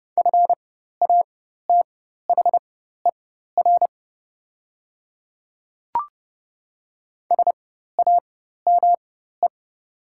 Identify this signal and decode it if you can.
Morse code, heard as a keyed tone, about 30 wpm, spaced as for 14 wpm, 700 Hz. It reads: FATHER SAME
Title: 4x Character Spacing - Top 200 Words - 30wpm